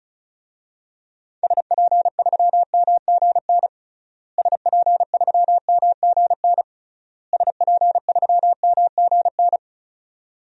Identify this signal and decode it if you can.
Morse code, heard as a keyed tone, about 36 wpm, 700 Hz. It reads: SP3MGN SP3MGN SP3MGN
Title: DXCC-100 Call Signs - 6 Characters - Repeated 3x - 35wpm